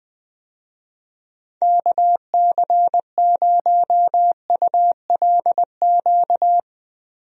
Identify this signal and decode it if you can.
Morse code, heard as a keyed tone, about 21 wpm, 700 Hz. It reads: KC0ULQ
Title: USA Call Signs - 6 Characters - Repeated 3x - 20wpm